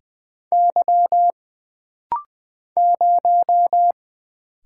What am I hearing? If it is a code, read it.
Morse code, heard as a keyed tone, about 20 wpm, 700 Hz. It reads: Y 0
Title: Single Letter-Number Alt - Rapid-Fire 20wpm